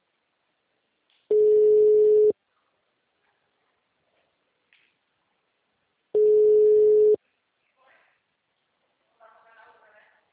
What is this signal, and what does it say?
Ringback tone (Europe, 425 Hz), with speech.